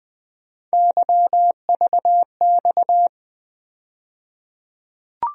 Morse code, heard as a keyed tone, about 21 wpm, 700 Hz. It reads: YVX
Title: USA Call Sign Postfixes - 3 Letters 20wpm